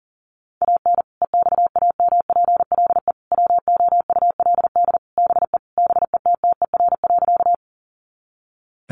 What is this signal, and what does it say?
Morse code, heard as a keyed tone, about 40 wpm, 700 Hz. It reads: AN EXAMPLE WOULD BE BETTER.
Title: Sentences from Top 400 Words 40wpm